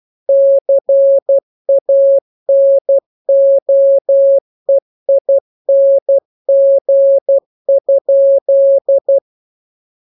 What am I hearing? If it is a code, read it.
Morse code, heard as a keyed tone, about 12 wpm, 550 Hz. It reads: CANOEING?
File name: jokes / joke 1 12 0000